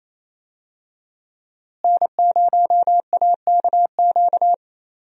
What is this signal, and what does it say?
Morse code, heard as a keyed tone, about 28 wpm, 700 Hz. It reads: N0AKQ
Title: USA Call Signs - 5 Characters - Repeated 3x - 28wpm